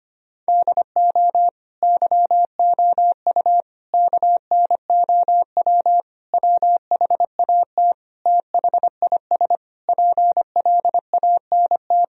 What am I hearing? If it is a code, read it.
Morse code, heard as a keyed tone, about 25 wpm, 700 Hz. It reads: DO YOU KNOW WHAT THIS PLANT
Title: Sentences from Top 1000 Words - Review Part 1 - 25wpm